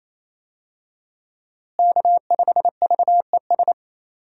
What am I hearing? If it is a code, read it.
Morse code, heard as a keyed tone, about 27 wpm, 700 Hz. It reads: K5VES
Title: USA Call Signs - 5 Characters - Repeated 3x - 28wpm